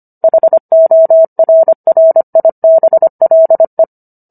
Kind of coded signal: Morse code